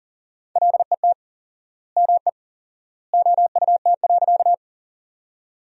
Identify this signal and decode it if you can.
Morse code, heard as a keyed tone, about 40 wpm, 700 Hz. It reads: LET ME OUT.
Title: 4x Word-Spacing - Sentences from Top 300 Words - 40wpm